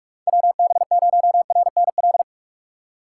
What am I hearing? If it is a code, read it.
Morse code, heard as a keyed tone, about 41 wpm, 700 Hz. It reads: WB0RNL
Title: USA Call Signs - 3x Character Spacing - 45wpm